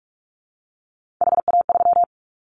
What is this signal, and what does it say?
Morse code, heard as a keyed tone, about 44 wpm, 700 Hz. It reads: HA3
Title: Callsign Prefixes for Top 25 DXCC - 45wpm